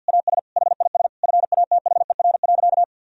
Morse code, heard as a keyed tone, about 50 wpm, 700 Hz. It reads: AS HIS FATHER.
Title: Sentences from Top 1000 Words - Review Part 4 - 50wpm